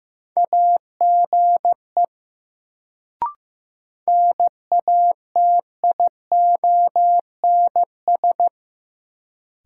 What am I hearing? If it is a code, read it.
Morse code, heard as a keyed tone, about 15 wpm, 700 Hz. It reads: AGE NATIONS
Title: Sets of 3 Words - New Word Drill - 15wpm